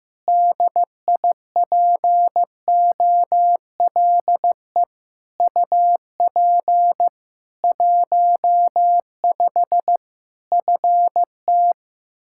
Morse code, heard as a keyed tone, about 15 wpm, 700 Hz. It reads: DIPOLE UP 15 FT
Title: Practice QSO - Exchange 2 - Info - 15wpm